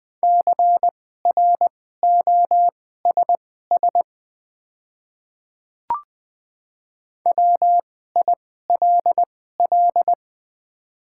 Morse code, heard as a keyed tone, about 20 wpm, spaced as for 14 wpm, 700 Hz. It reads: CROSS WILL
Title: 2x Character Spacing - Top 500 Words - 20wpm